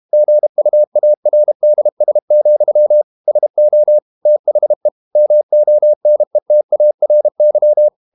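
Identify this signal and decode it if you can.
Morse code, heard as a keyed tone, about 32 wpm, 600 Hz. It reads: GUARDS, SO THE MONETARY
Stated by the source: Morse code practice at 32 wpm from CW med Gnister, reading Ken Follett's novel Never in English.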